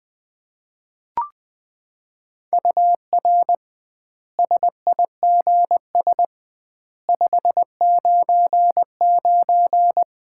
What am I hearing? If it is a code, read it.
Morse code, heard as a keyed tone, about 20 wpm, 700 Hz. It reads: UR SIGS 599
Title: Common QSO Phrases - 2x Word-Spacing - 20wpm